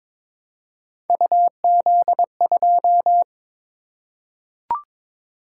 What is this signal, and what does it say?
Morse code, heard as a keyed tone, about 22 wpm, 700 Hz. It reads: UZ2
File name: practice / 22wpm / Three Letter-Number Combinations - 22wpm